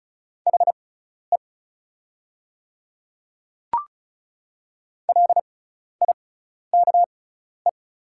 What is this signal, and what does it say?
Morse code, heard as a keyed tone, about 33 wpm, spaced as for 12 wpm, 700 Hz. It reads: HE LIKE
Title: 6x Character Spacing - Top 100 Words - 35wpm